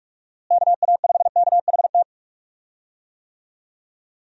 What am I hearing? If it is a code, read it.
Morse code, heard as a keyed tone, about 41 wpm, 700 Hz. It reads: KA5KHT
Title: USA Call Signs - 6 Characters - Repeated 3x - 45wpm